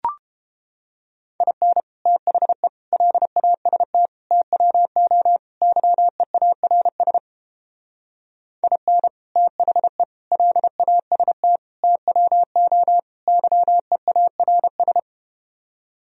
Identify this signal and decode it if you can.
Morse code, heard as a keyed tone, about 29 wpm, 700 Hz. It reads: IN THE LAST TWO YEARS IN THE LAST TWO YEARS